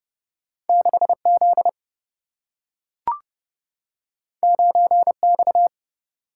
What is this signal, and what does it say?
Morse code, heard as a keyed tone, about 30 wpm, 700 Hz. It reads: 6 Z 9 X
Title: Single-Letter Single-Number 30wpm